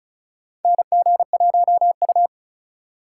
Morse code, heard as a keyed tone, about 36 wpm, 700 Hz. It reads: NG1U
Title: USA Call Signs - 4 Characters - 3x Character Spacing - 35wpm